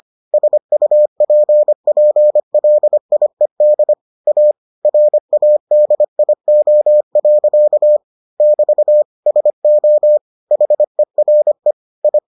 Morse code, heard as a keyed tone, about 25 wpm, 600 Hz. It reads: SUPPLIED A RADIO. <BT> SO HERE I